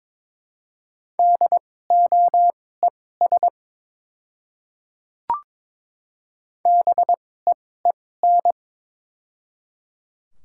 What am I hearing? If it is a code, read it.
Morse code, heard as a keyed tone, about 22 wpm, spaced as for 16 wpm, 700 Hz. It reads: DOES BEEN
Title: 2x Character Spacing - Top 300 Words - 22wpm